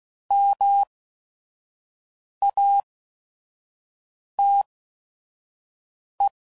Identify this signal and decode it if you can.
Morse code, heard as a keyed tone, about 16 wpm, spaced as for 5 wpm, 800 Hz. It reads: MATE